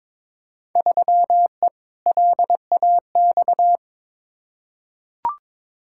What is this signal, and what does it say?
Morse code, heard as a keyed tone, about 22 wpm, 700 Hz. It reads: 3E LAX
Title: Field Day Exchanges Encore - 22wpm